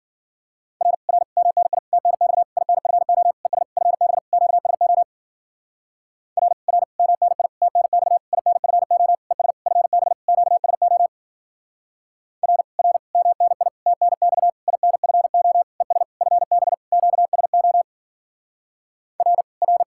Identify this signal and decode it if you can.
Morse code, heard as a keyed tone, about 50 wpm, 700 Hz. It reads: R R MNI TNX INFO ES FB QSO R R MNI TNX INFO ES FB QSO R R MNI TNX INFO ES FB QSO R R
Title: Speed-Racing Common QSO Phrases Encore - 40wpm